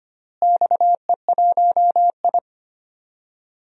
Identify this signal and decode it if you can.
Morse code, heard as a keyed tone, about 25 wpm, 700 Hz. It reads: XE1I
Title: DXCC-100 Call Signs - 4 Characters - Repeated 3x - 25wpm